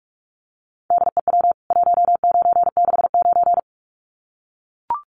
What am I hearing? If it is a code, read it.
Morse code, heard as a keyed tone, about 44 wpm, 700 Hz. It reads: DEW 1969
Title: CWT Exchange 45wpm